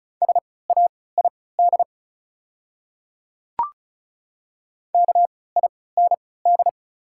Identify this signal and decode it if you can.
Morse code, heard as a keyed tone, about 34 wpm, spaced as for 20 wpm, 690 Hz. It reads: SAID KIND
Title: 3x Character Spacing - Top 300 Words - 35wpm